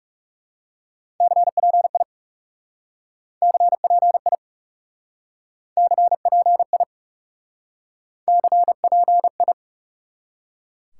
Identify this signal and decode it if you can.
Morse code, heard as a keyed tone, about 38 wpm, 700 Hz. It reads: CPI CPI CPI CPI